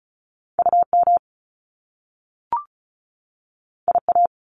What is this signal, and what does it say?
Morse code, heard as a keyed tone, about 36 wpm, 700 Hz. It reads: U M I A
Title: Top Two Letter Combinations in Words 35wpm